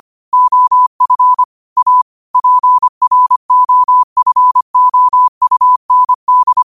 Morse code, heard as a keyed tone, about 25 wpm, 1000 Hz. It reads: OF A PROFOUND
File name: qotdfiles / 25wpm